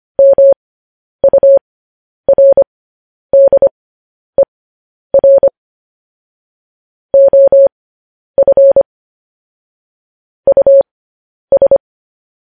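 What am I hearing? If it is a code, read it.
Morse code, heard as a keyed tone, about 25 wpm, spaced as for 10 wpm, 550 Hz. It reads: MURDER OF US
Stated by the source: Morse code news headlines sent at 10 words per minute.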